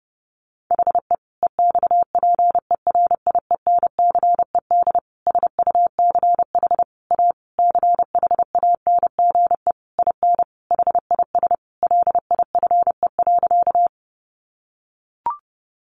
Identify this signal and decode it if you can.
Morse code, heard as a keyed tone, about 30 wpm, 700 Hz. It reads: HE EXPERIENCED SUCH A CHANGE IN HIS LIFE.
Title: Sentences from Top 1100 Words 30wpm